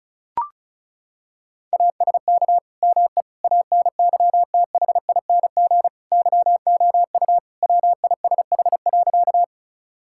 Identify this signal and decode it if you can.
Morse code, heard as a keyed tone, about 36 wpm, 700 Hz. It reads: ASK ME ANYTHING YOU WISH.
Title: Sentences from Top 1600 Words 35wpm